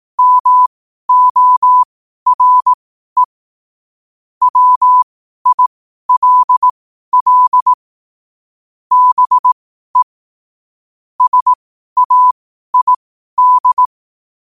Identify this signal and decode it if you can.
Morse code, heard as a keyed tone, about 18 wpm, spaced as for 12 wpm, 1000 Hz. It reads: MORE WILL BE SAID
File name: qotdfiles / 12wpm